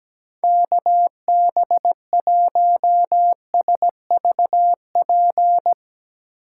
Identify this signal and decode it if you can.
Morse code, heard as a keyed tone, about 17 wpm, 700 Hz. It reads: KB1SVP
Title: USA Call Signs - 6 Characters - 6x Character Spacing - 17wpm